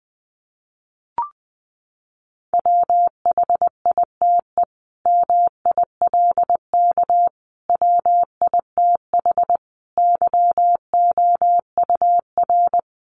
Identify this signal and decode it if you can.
Morse code, heard as a keyed tone, about 20 wpm, 700 Hz. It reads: WHITE MILK WITH YOUR
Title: Sentences from Top 1500 Words 20wpm